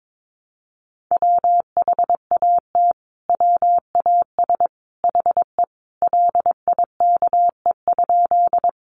Morse code, heard as a keyed tone, about 22 wpm, 700 Hz. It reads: WHAT WAS HE LIKE?
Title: Sentences from Top 100 Words 22wpm